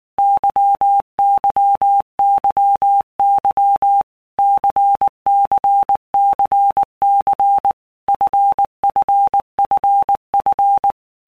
Morse code, heard as a keyed tone, about 19 wpm, 800 Hz. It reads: YYYY CCCC FFFF
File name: Basic Letter Groups By Blocks / Groups@18 / LFCYQ@18